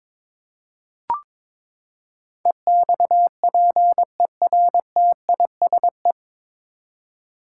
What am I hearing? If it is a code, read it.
Morse code, heard as a keyed tone, about 22 wpm, 700 Hz. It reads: EXPERTISE
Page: Morse Code Ninja practice 22wpm